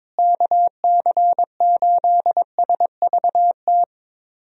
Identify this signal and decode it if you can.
Morse code, heard as a keyed tone, about 22 wpm, 700 Hz. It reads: KC8SVT
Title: USA Call Signs - 6 Characters - Repeated 3x - 22wpm